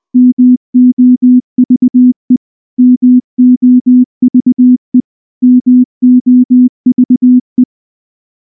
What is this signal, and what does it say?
Morse code, heard as a keyed tone, about 20 wpm, 250 Hz. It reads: MOVE MOVE MOVE
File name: morse-training-20WPM